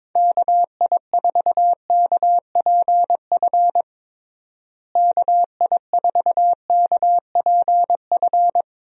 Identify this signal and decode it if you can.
Morse code, heard as a keyed tone, about 22 wpm, 700 Hz. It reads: KI4KPF KI4KPF
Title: USA Call Signs - 6 Characters - Repeated 3x - 22wpm